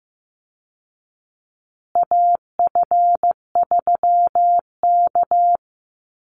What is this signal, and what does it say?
Morse code, heard as a keyed tone, about 15 wpm, 700 Hz. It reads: AF3K
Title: Call Signs Repeated 2x - 15wpm